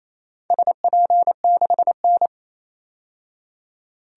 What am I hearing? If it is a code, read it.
Morse code, heard as a keyed tone, about 27 wpm, 700 Hz. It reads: SP6N
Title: DXCC-100 Call Signs - 4 Characters - 28wpm